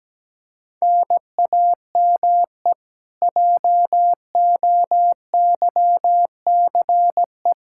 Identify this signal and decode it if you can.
Morse code, heard as a keyed tone, about 17 wpm, 700 Hz. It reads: NAME JOYCE
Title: K1USN SST QSO Practice 17wpm